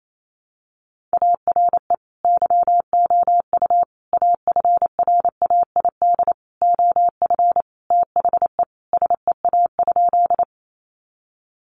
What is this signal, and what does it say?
Morse code, heard as a keyed tone, about 28 wpm, 700 Hz. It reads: ARE YOU AFRAID OF THE SEA?
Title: Sentences from Top 1500 Words 28wpm